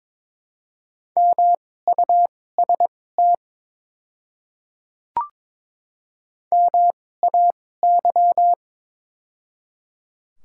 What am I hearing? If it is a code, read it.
Morse code, heard as a keyed tone, about 22 wpm, spaced as for 16 wpm, 700 Hz. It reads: MUST MAY
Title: 2x Character Spacing - Top 200 Words - 22wpm